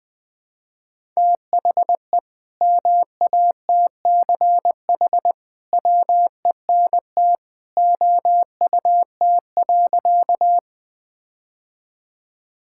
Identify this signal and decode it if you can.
Morse code, heard as a keyed tone, about 21 wpm, 700 Hz. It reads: THE MATCH WENT OUT.
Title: Sentences from Top 1600 Words 20wpm